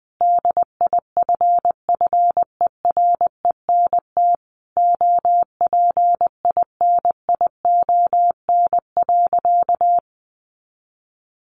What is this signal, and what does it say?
Morse code, heard as a keyed tone, about 20 wpm, 700 Hz. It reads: DIFFERENT OPINION.
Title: Sentences from Top 1200 Words 20wpm